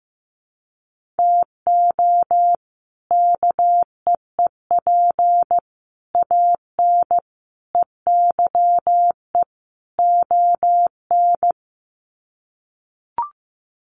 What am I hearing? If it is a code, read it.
Morse code, heard as a keyed tone, about 15 wpm, 700 Hz. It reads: TO KEEP AN EYE ON
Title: Sets of 5 Words Encore 15wpm